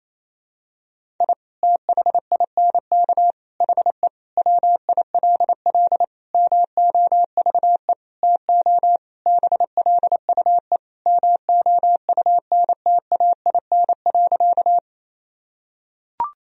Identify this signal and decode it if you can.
Morse code, heard as a keyed tone, about 28 wpm, 700 Hz. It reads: I THINK HE WILL MOVE TO BLUE MOUNTAIN.
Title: Sentences from Top 1000 Words - Review Part 3 - 28wpm